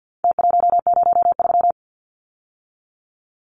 Morse code, heard as a keyed tone, about 50 wpm, 700 Hz. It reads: T103
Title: Numbers - Contest Style - 4-Digit 50wpm